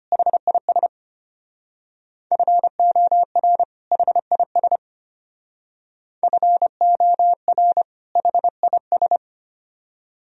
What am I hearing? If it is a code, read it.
Morse code, heard as a keyed tone, about 30 wpm, 700 Hz. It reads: HIS FOR HIS FOR HIS
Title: Speed-Racing - Sets of 2 Words 25wpm